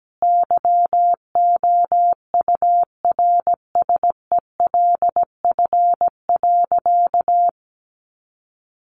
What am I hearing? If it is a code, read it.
Morse code, heard as a keyed tone, about 17 wpm, 700 Hz. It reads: YOURSELF.
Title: Sentences from Top 600 Words 17wpm